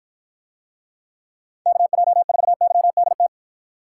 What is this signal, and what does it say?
Morse code, heard as a keyed tone, about 53 wpm, 700 Hz. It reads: KJ4YDT